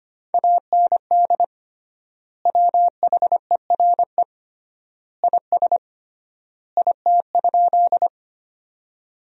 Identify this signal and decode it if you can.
Morse code, heard as a keyed tone, about 25 wpm, 700 Hz. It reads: AND WHERE IS IT?
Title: 3x Word-Spacing - Sentences from Top 200 Words - 25wpm